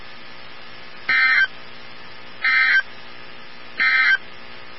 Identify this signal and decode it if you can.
Emergency Alert System (SAME) tones, heard as harsh buzzing bursts. It is heard from 1.0 to 4.3 s.
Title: NOAA Weather Radio 162.550